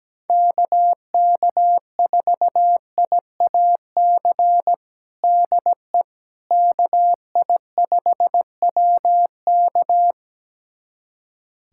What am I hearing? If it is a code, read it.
Morse code, heard as a keyed tone, about 17 wpm, 700 Hz. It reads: KK4IAC DE KI5WK
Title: USA Callsign DE Callsign 17wpm